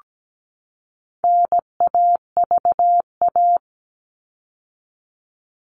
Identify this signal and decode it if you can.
Morse code, heard as a keyed tone, about 17 wpm, 700 Hz. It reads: NAVA
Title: All Cut Numbers - 4-Digit 17wpm